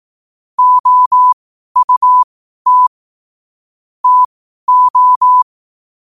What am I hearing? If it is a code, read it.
Morse code, heard as a keyed tone, about 18 wpm, spaced as for 12 wpm, 1000 Hz. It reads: OUT TO
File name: qotdfiles / 12wpm